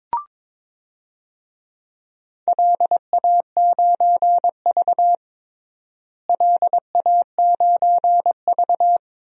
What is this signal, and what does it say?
Morse code, heard as a keyed tone, about 21 wpm, 700 Hz. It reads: LA9V LA9V
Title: DXCC-100 Call Signs - 4 Characters - Repeated 3x - 22wpm